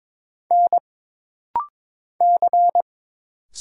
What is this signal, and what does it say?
Morse code, heard as a keyed tone, about 22 wpm, 700 Hz. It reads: N C
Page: This rapid-fire timing is geared to push your ICR (Instant Character Recognition) skills.